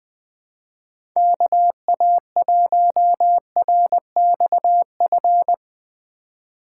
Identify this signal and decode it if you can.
Morse code, heard as a keyed tone, about 20 wpm, 700 Hz. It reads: KA1RXF
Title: USA Call Signs - 6 Characters - 4x Character Spacing - 20wpm